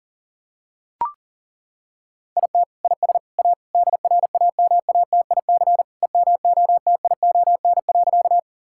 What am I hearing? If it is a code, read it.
Morse code, heard as a keyed tone, about 40 wpm, 700 Hz. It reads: IT IS A DRAMATIC EMOTION.
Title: Sentences from Top 1600 Words 40wpm